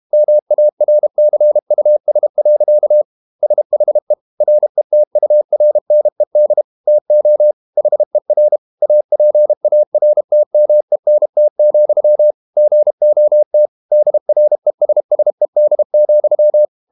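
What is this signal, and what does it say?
Morse code, heard as a keyed tone, about 32 wpm, 600 Hz. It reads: MARCUS. SHE RETURNED TO HER APARTMENT, GOT DRESSED,
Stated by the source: Morse code practice at 32 wpm from CW med Gnister, reading Ken Follett's novel Never in English.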